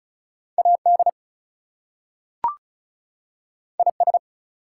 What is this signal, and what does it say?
Morse code, heard as a keyed tone, about 34 wpm, 700 Hz. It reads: AD IS
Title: Two Letter Words - 35wpm